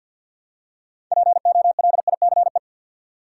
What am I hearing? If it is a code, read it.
Morse code, heard as a keyed tone, about 53 wpm, 700 Hz. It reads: POLICE
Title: Six Letter Words - 50wpm